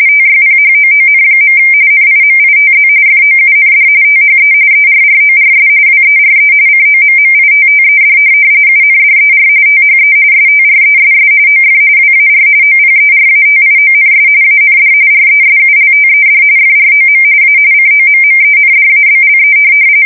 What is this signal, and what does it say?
RTTY, about 45.45 baud. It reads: TYPHOON. NIMITZ FOUND THE ERRORS WERE COMMITTED UNDER THE STRESS OF WAR OPERATIONS AND RESULTED FROM A COMMENDABLE
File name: fsk170 / USS-New-Jersey